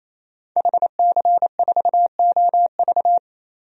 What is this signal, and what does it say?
Morse code, heard as a keyed tone, about 28 wpm, 700 Hz. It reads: HC4OV
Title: DXCC-100 Call Signs - 5 Characters - 28wpm